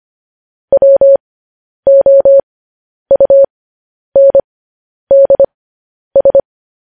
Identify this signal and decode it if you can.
Morse code, heard as a keyed tone, about 25 wpm, spaced as for 10 wpm, 550 Hz. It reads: WOUNDS
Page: news-headlines-in-morse-code-at-10-wpm